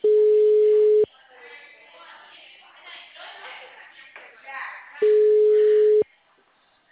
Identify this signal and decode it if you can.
Ringback tone (Europe, 425 Hz), with speech.